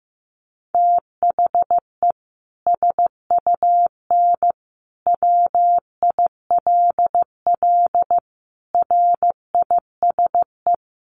Morse code, heard as a keyed tone, about 15 wpm, 700 Hz. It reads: THE SUN WILL RISE
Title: Sentences from Top 900 Words 15wpm